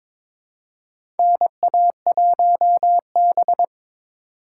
Morse code, heard as a keyed tone, about 22 wpm, 700 Hz. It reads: NA1B
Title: USA Call Signs - 4 Characters - 3x Character Spacing - 22wpm